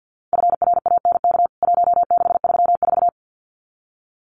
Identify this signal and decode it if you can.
Morse code, heard as a keyed tone, about 50 wpm, 700 Hz. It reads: FRANK 1634
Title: CWT Exchange 50wpm